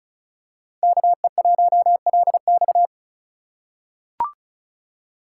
Morse code, heard as a keyed tone, about 36 wpm, 700 Hz. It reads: KE1LX
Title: USA Call Signs - 3x Character Spacing - 35wpm